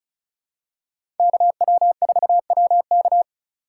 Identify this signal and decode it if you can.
Morse code, heard as a keyed tone, about 36 wpm, 700 Hz. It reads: KW4WK
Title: USA Call Signs - 5 Characters - 6x Character Spacing - 35wpm